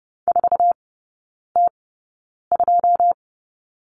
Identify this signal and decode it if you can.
Morse code, heard as a keyed tone, about 30 wpm, spaced as for 9 wpm, 700 Hz. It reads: 4T2
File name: course / 30:10wpm / 087 Lesson 15e - Number 5 - Three Characters 30wpm